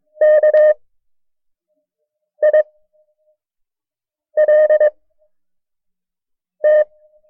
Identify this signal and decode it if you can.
Morse code, heard as a keyed tone, about 22 wpm, spaced as for 5 wpm, 610 Hz. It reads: KILT
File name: FreeFiles / CW-practice / KB6MT-tape1-side2